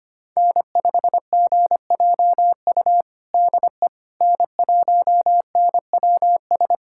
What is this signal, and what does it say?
Morse code, heard as a keyed tone, about 25 wpm, 700 Hz. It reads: N5GJU DE N1NWS
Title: USA Callsign DE Callsign - 5 Characters - 25wpm